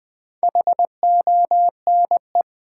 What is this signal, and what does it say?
Morse code, heard as a keyed tone, about 20 wpm, 700 Hz. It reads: HONE
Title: Build A Word - 2 Letters At A Time - 6-Letter Words - 20wpm